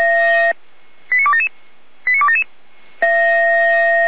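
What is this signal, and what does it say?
Five-tone selective call (ZVEI), heard as quick runs of five tones, tones of about 70 ms. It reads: 89277 89277
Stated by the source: ZVEI five-tone selective call 89100 received on SDR.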